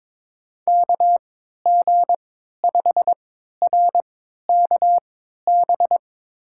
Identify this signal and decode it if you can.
Morse code, heard as a keyed tone, about 22 wpm, spaced as for 13 wpm, 700 Hz. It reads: KG5RKB
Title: USA Call Signs - 6 Characters - 3x Character Spacing - 22wpm